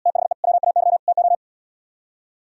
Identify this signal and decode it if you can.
Morse code, heard as a keyed tone, about 77 wpm, 700 Hz. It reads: THE WAY TO